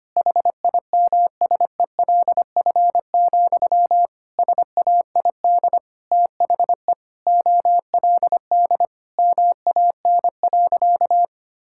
Morse code, heard as a keyed tone, about 25 wpm, 700 Hz. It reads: HIMSELF, SAID THE OLD MAN.
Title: Sentences from Top 1600 Words 25wpm